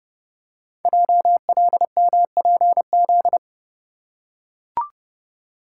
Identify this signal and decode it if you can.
Morse code, heard as a keyed tone, about 30 wpm, 700 Hz. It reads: JLMPZ
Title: Code Groups - 5 Letters - 30wpm